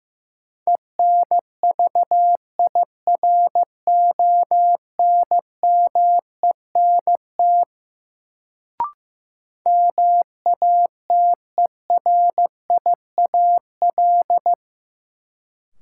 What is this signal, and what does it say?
Morse code, heard as a keyed tone, about 15 wpm, 700 Hz. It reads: ENVIRONMENT MATERIAL
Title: Sentences from Top 1000 Words - Review Part 5 - New Word Drill - 15wpm